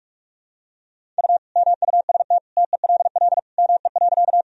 Morse code, heard as a keyed tone, about 41 wpm, 700 Hz. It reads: U MUST TELL ME.
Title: Sentences from Top 200 Words and Abbreviations 45wpm